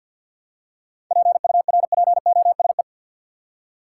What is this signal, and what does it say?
Morse code, heard as a keyed tone, about 46 wpm, 700 Hz. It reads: PURPOSE
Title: Seven Letter Words - 50wpm